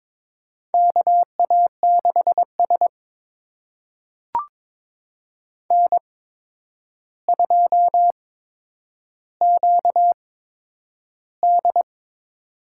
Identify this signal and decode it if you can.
Morse code, heard as a keyed tone, about 22 wpm, spaced as for 9 wpm, 700 Hz. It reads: KA6S N2QD
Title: USA Call Signs - 4 Characters - 8x Character Spacing - 22wpm